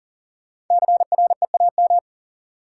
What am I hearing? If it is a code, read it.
Morse code, heard as a keyed tone, about 40 wpm, 700 Hz. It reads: CREAM
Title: Sentences from Top 1500 Words - New Word Drill Alt - 40wpm